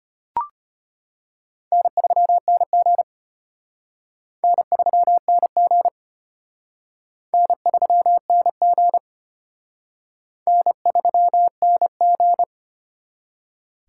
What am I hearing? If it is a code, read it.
Morse code, heard as a keyed tone, about 32 wpm, 700 Hz. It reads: N3NG N3NG N3NG N3NG